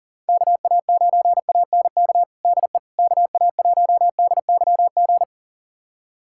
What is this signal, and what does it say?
Morse code, heard as a keyed tone, about 37 wpm, 700 Hz. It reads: KA9ANK DE KA1DYG
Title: USA Callsign DE Callsign - 6 Characters - 40wpm